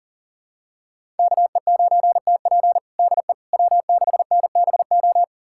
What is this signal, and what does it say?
Morse code, heard as a keyed tone, about 40 wpm, 700 Hz. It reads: KE9TP DE W6NBO
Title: USA Callsign DE Callsign - 5 Characters - 40wpm